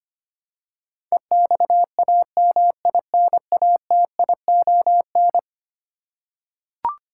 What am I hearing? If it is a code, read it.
Morse code, heard as a keyed tone, about 25 wpm, 700 Hz. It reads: EXAMINATION
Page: morse-code-ninja-practice-25wpm